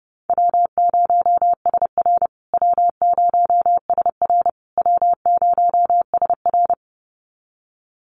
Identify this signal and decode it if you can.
Morse code, heard as a keyed tone, about 30 wpm, 700 Hz. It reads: W0SR W0SR W0SR
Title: USA Call Signs Repeated 3x 30wpm